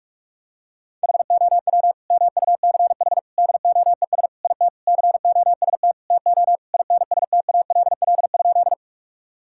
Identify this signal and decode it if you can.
Morse code, heard as a keyed tone, about 44 wpm, 700 Hz. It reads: HOW MUCH DOES IT COST TO INSTALL?
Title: Sentences from Top 1600 Words 45wpm